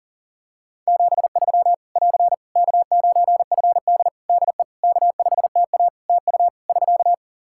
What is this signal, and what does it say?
Morse code, heard as a keyed tone, about 37 wpm, 700 Hz. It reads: 73 <AR> K9FD DE K5TA TU <SK>
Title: K1USN SST QSO Practice 40wpm